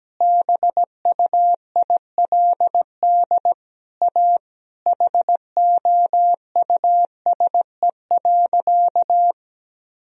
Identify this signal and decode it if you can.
Morse code, heard as a keyed tone, about 17 wpm, 700 Hz. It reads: BUILD A HOUSE.
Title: Sentences from Top 1000 Words - Review Part 4 - 17wpm